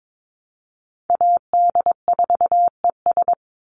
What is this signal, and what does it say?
Morse code, heard as a keyed tone, about 22 wpm, 700 Hz. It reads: AD4ES
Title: Call Signs Repeated 3x - 22wpm